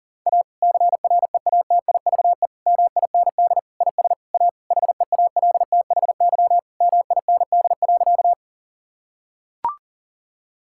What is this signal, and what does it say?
Morse code, heard as a keyed tone, about 40 wpm, 700 Hz. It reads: A CREATIVE MIND IS A HEALTHY MIND.
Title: Sentences from Top 1000 Words - Review Part 5 - 40wpm